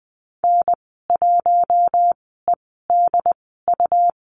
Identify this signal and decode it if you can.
Morse code, heard as a keyed tone, about 20 wpm, spaced as for 15 wpm, 700 Hz. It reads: N1EDU
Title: 147 Lesson 24g - Letter y - Call Signs 20wpm